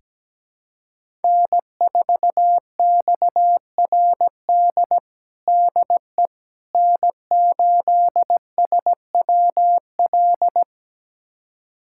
Morse code, heard as a keyed tone, about 17 wpm, 700 Hz. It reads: N4XRD DE N8SWL
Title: USA Callsign DE Callsign 17wpm